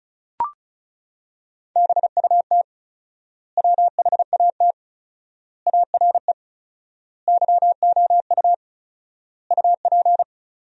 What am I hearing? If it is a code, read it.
Morse code, heard as a keyed tone, about 34 wpm, 700 Hz. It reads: BUT WHAT ARE YOU UP
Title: 4x Word-Spacing - Sentences from Top 100 Words - 35wpm